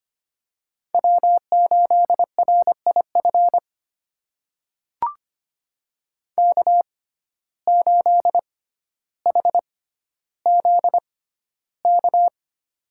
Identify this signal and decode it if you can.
Morse code, heard as a keyed tone, about 25 wpm, spaced as for 13 wpm, 700 Hz. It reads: W8RIF K8HZK
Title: USA Call Signs - 5 Characters - 6x Character Spacing - 25wpm